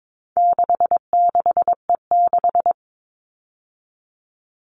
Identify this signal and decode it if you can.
Morse code, heard as a keyed tone, about 22 wpm, 700 Hz. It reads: 66E6
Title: All Cut Numbers - 4-Digit 22wpm